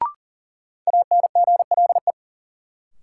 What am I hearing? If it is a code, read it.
Morse code, heard as a keyed tone, about 40 wpm, 700 Hz. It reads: ANGLE
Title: Sentences from Top 1500 Words - New Word Drill Alt - 40wpm